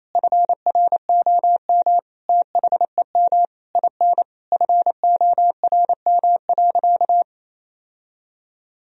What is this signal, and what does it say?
Morse code, heard as a keyed tone, about 28 wpm, 700 Hz. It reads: FROM THEM IN FORM.
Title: Sentences from Top 1000 Words - Review Part 1 - 28wpm